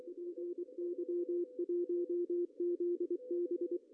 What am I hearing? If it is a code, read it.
Morse code, heard as a keyed tone, about 23 wpm, 350 Hz, with radio noise. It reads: PY1ZB
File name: CW / cqww-cw-k5tr